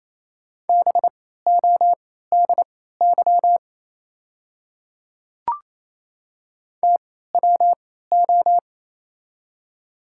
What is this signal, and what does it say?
Morse code, heard as a keyed tone, about 28 wpm, spaced as for 16 wpm, 700 Hz. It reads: BODY TWO
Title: 3x Character Spacing - Top 500 Words - 28wpm